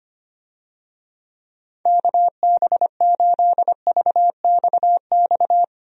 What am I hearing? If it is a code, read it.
Morse code, heard as a keyed tone, about 25 wpm, 700 Hz. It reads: KB8VXX
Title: USA Call Signs - 6 Characters - Repeated 2x - 25wpm